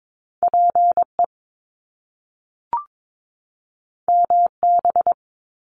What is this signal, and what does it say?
Morse code, heard as a keyed tone, about 22 wpm, 700 Hz. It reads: PE MB
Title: Canadian Provinces and Territories 22wpm